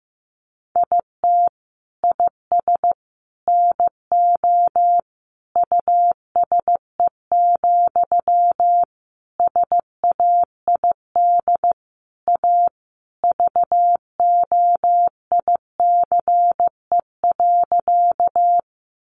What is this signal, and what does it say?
Morse code, heard as a keyed tone, about 15 wpm, 700 Hz. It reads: IT IS NO USE, SAID A VOICE.
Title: Sentences from Top 500 Words 15wpm